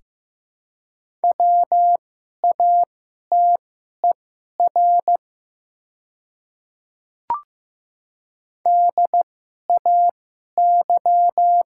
Morse code, heard as a keyed tone, about 15 wpm, spaced as for 11 wpm, 700 Hz. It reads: WATER DAY
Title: 2x Character Spacing - Top 200 Words - 15wpm